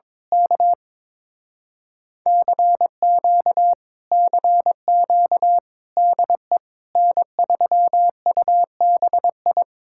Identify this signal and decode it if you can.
Morse code, heard as a keyed tone, about 22 wpm, 700 Hz. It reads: K CQ CQ DE N3UBI
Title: Speed-Racing Common QSO Phrases - 22wpm